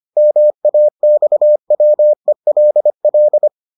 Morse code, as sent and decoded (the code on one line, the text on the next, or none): -- .- -..- .-- . .-.. .-..
MAXWELL